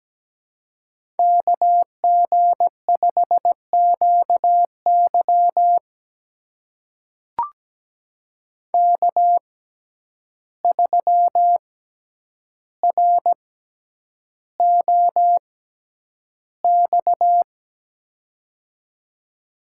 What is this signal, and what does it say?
Morse code, heard as a keyed tone, about 17 wpm, spaced as for 9 wpm, 700 Hz. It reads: KG5QY K3ROX
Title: USA Call Signs - 5 Characters - 6x Character Spacing - 17wpm